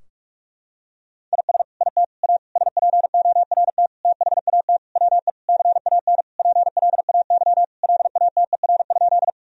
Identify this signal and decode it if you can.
Morse code, heard as a keyed tone, about 46 wpm, 700 Hz. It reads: IS IT A SPORT THAT WE CAN PLAY LATER?
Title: Sentences from Top 1000 Words - Review Part 4 - 45wpm